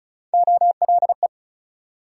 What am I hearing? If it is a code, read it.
Morse code, heard as a keyed tone, about 36 wpm, 700 Hz. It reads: OLE
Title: USA Call Sign Postfixes - 3 Letters 35wpm